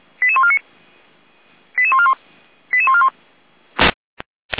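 Five-tone selective call (ZVEI), heard as quick runs of five tones, tones of about 70 ms. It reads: 88149 88141 88141
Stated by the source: ZVEI five-tone selective call received on an SDR.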